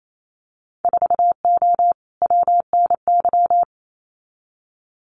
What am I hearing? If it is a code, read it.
Morse code, heard as a keyed tone, about 28 wpm, 700 Hz. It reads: 4O WNY
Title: Winter Field Day Exchanges Encore 28wpm